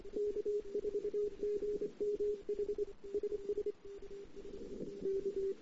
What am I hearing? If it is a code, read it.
Morse code, heard as a keyed tone, about 26 wpm, 410 Hz, with radio noise. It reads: K4GMH SS K
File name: CW / 2008-ss-cw-k5tr-op-n5rz -08:00